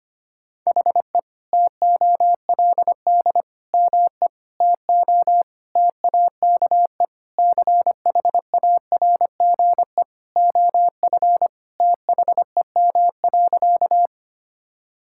Morse code, heard as a keyed tone, about 25 wpm, 700 Hz. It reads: HE TOLD ME TO TAKE CHARGE OF THEM.
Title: Sentences from Top 1000 Words - Review Part 6 - 25wpm